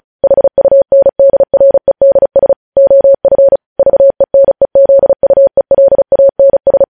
Morse code, heard as a keyed tone, about 36 wpm, 550 Hz. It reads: HUNDREDS OF VENEZUELANS
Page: News Headlines in Morse Code at 35 WPM